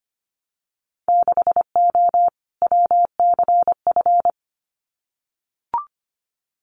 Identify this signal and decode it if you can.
Morse code, heard as a keyed tone, about 25 wpm, 700 Hz. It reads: 6O WCF
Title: Winter Field Day Exchanges Encore 25wpm